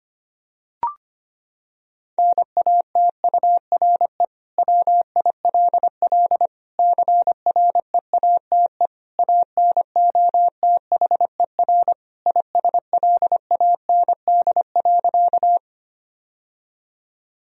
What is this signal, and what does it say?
Morse code, heard as a keyed tone, about 25 wpm, 700 Hz. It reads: NATURE WILL CREATE ANOTHER ISLAND.
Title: Sentences from Top 1000 Words - Review Part 3 - 25wpm